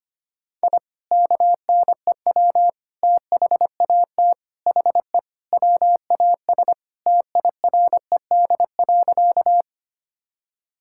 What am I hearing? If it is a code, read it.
Morse code, heard as a keyed tone, about 25 wpm, 700 Hz. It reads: I KNEW THAT HE WAS TIRED.
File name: practice / 25wpm / Sentences with 3-Word Phrases - 25wpm